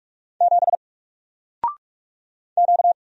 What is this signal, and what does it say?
Morse code, heard as a keyed tone, about 46 wpm, 700 Hz. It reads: 7 Q
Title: Single Letter-Number 45wpm